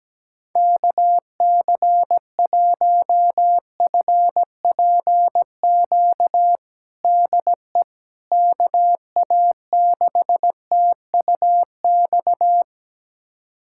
Morse code, heard as a keyed tone, about 17 wpm, 700 Hz. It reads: KC1FPQ DE KA6TUX
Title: USA Callsign DE Callsign 17wpm